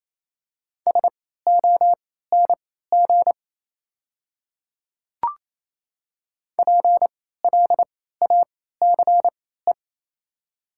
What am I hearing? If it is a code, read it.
Morse code, heard as a keyed tone, about 28 wpm, spaced as for 16 wpm, 700 Hz. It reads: SONG PLACE
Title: 3x Character Spacing - Top 500 Words - 28wpm